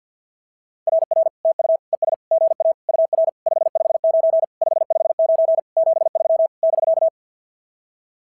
Morse code, heard as a keyed tone, about 50 wpm, 690 Hz. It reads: RR TU ES GA UR 559 559 73 <BK>
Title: SOTA QSO Practice 50wpm